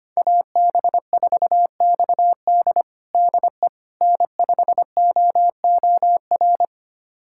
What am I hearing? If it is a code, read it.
Morse code, heard as a keyed tone, about 25 wpm, 700 Hz. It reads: AB4XD DE N5OOR
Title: USA Callsign DE Callsign - 5 Characters - 25wpm